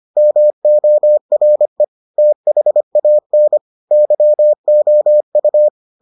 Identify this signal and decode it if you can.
Morse code, heard as a keyed tone, about 25 wpm, 600 Hz. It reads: MORE THAN YOU